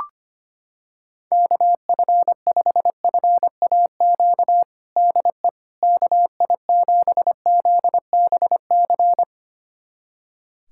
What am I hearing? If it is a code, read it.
Morse code, heard as a keyed tone, about 25 wpm, 700 Hz. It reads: KF5FAQ DE KI7ZBC
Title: USA Callsign DE Callsign 25wpm